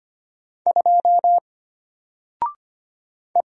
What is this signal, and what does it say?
Morse code, heard as a keyed tone, about 25 wpm, 700 Hz. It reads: 2 E
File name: Single-Letter-Number-25wpm